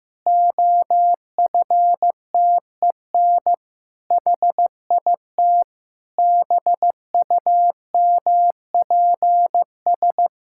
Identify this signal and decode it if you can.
Morse code, heard as a keyed tone, about 15 wpm, 700 Hz. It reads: OFTEN HIT BUMPS
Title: Pangram Challenge v2 - 15wpm